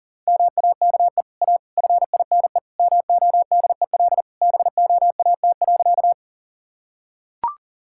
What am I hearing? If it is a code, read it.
Morse code, heard as a keyed tone, about 37 wpm, 700 Hz. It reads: MAKE A FINE MODEL BOAT.
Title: Sentences from Top 1000 Words - Review Part 5 - 40wpm